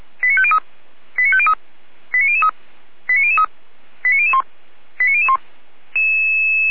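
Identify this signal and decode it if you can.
Five-tone selective call (ZVEI), heard as quick runs of five tones, tones of about 70 ms. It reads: 89503 89503 89003 89003 89001 89001
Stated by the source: ZVEI five-tone call, code 89150, received by SDR.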